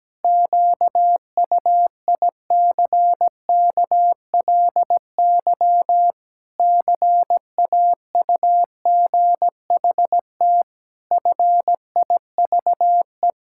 Morse code, heard as a keyed tone, about 17 wpm, 700 Hz. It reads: QUICKLY CAUGHT FIVE
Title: Pangram Challenge v2 - 17wpm